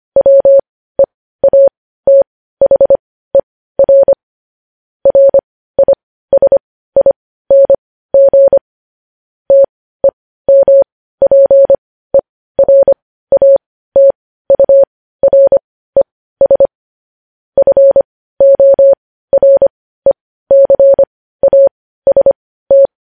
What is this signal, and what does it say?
Morse code, heard as a keyed tone, about 25 wpm, spaced as for 15 wpm, 550 Hz. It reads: WEATHER RISING TEMPERATURES FORECAST
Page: news headlines in morse code at 15 wpm